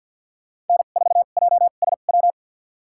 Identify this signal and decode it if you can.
Morse code, heard as a keyed tone, about 52 wpm, spaced as for 37 wpm, 700 Hz. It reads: N4JSW
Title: USA Call Signs - 2x Character Spacing - 50wpm